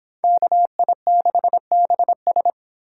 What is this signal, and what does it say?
Morse code, heard as a keyed tone, about 26 wpm, 700 Hz. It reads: KI6BS